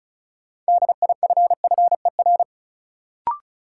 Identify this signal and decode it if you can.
Morse code, heard as a keyed tone, about 34 wpm, 700 Hz. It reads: DIFFER
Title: Sentences from Top 1100 Words - New Word Drill Alt - 35wpm